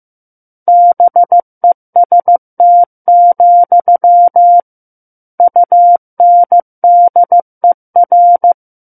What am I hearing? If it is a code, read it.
Morse code, heard as a keyed tone, about 15 wpm, 700 Hz. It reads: BEST, UNDER
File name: Book One / 06-Chapter 6 - 15 WPM